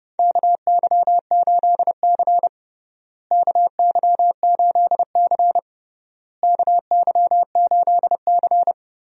Morse code, heard as a keyed tone, about 30 wpm, 700 Hz. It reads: KY8C KY8C KY8C
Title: USA Call Signs - 4 Characters - Repeated 3x - 30wpm